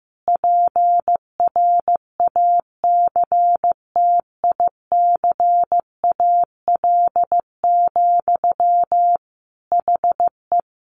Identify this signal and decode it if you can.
Morse code, heard as a keyed tone, about 15 wpm, 700 Hz. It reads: PRACTICAL, HE
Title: Sentences from Top 1400 Words 15wpm